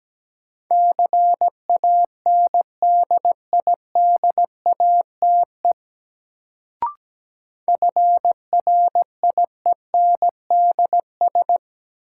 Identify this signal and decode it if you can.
Morse code, heard as a keyed tone, about 17 wpm, 700 Hz. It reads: CANDIDATE FRIENDS
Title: Sentences from Top 1600 Words - New Word Drill Alt - 17wpm